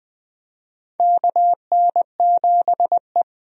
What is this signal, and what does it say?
Morse code, heard as a keyed tone, about 20 wpm, 700 Hz. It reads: KN7E
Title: USA Call Signs - 4 Characters - 20wpm